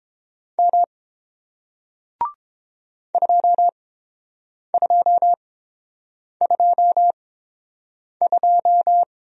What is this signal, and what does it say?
Morse code, heard as a keyed tone, about 31 wpm, spaced as for 7 wpm, 700 Hz. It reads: M 2222